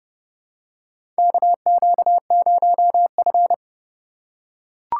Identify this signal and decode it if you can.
Morse code, heard as a keyed tone, about 31 wpm, 700 Hz. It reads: KQ0F
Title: USA Call Signs - 4 Characters - 4x Character Spacing - 30wpm